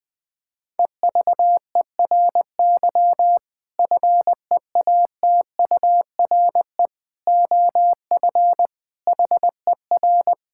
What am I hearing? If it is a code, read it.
Morse code, heard as a keyed tone, about 21 wpm, 700 Hz. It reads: EVERY FEATURE OF HER
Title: Sentences from Top 1000 Words - Review Part 5 - 20wpm